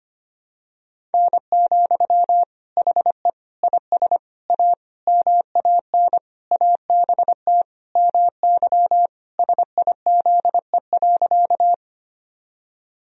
Morse code, heard as a keyed tone, about 25 wpm, 700 Hz. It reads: N, HE IS A MAN ABT MY SIZE.
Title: Sentences from Top 400 Words and Abbreviations 25wpm